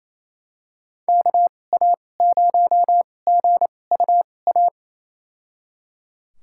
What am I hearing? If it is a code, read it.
Morse code, heard as a keyed tone, about 28 wpm, spaced as for 20 wpm, 700 Hz. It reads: KA0GUA
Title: USA Call Signs - 6 Characters - 2x Character Spacing - 28wpm